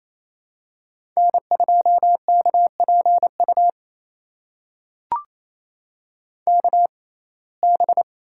Morse code, heard as a keyed tone, about 28 wpm, spaced as for 20 wpm, 700 Hz. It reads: N2KPU KB
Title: USA Call Signs - 6x Character Spacing - 28wpm